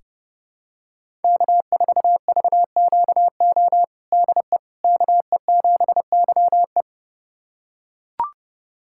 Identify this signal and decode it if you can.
Morse code, heard as a keyed tone, about 30 wpm, 700 Hz. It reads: K4VQO DE KE7YE
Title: USA Callsign DE Callsign - 5 Characters - 30wpm